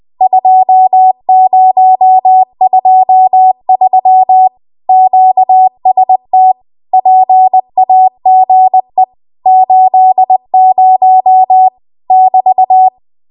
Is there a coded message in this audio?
Yes: Morse code from 0.2 to 12.9 s, heard as a keyed tone, about 20 wpm, 750 Hz. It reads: 2023 QST PAGE 80 <BT>